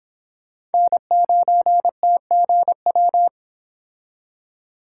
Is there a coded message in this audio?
Yes: Morse code from 0.7 to 3.3 s, heard as a keyed tone, about 30 wpm, 700 Hz. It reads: N9TGW